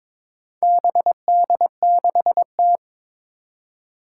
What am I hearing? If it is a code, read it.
Morse code, heard as a keyed tone, about 22 wpm, 700 Hz. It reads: BD6T
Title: Four Challenging Letter-Number Combinations 22wpm